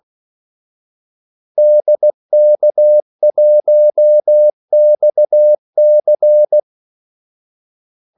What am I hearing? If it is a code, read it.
Morse code, heard as a keyed tone, about 16 wpm, 600 Hz. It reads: DK1XC
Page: Kaldesignaler 16 wpm